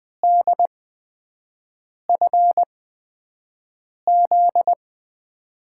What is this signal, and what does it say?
Morse code, heard as a keyed tone, about 20 wpm, spaced as for 5 wpm, 700 Hz. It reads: DFZ
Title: USA Call Signs - 6 Characters - 8x Character Spacing - 20wpm